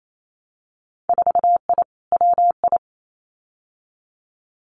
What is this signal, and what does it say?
Morse code, heard as a keyed tone, about 28 wpm, 700 Hz. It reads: 4I WI
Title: Winter Field Day Exchanges 28wpm